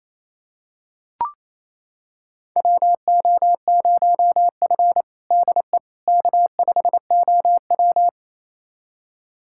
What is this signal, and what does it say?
Morse code, heard as a keyed tone, about 28 wpm, 700 Hz. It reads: WO0F DE K5OW
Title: USA Callsign DE Callsign - 4 Characters - 28wpm